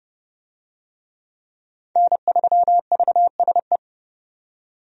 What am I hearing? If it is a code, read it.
Morse code, heard as a keyed tone, about 30 wpm, 700 Hz. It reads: N3VSE
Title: USA Call Signs - 5 Characters - 30wpm